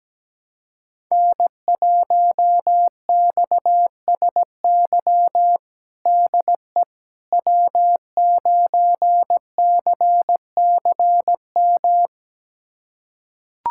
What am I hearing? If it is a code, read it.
Morse code, heard as a keyed tone, about 17 wpm, 700 Hz. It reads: N1XSY DE W9CCM
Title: USA Callsign DE Callsign - 5 Characters - 17wpm